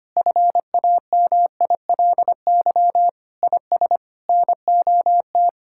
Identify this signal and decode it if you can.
Morse code, heard as a keyed tone, about 25 wpm, 700 Hz. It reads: FAMILY IS NOT
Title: Sentences from Top 1000 Words - Review Part 6 - 25wpm